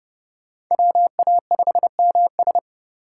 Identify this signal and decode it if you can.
Morse code, heard as a keyed tone, about 29 wpm, 700 Hz. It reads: WA5MS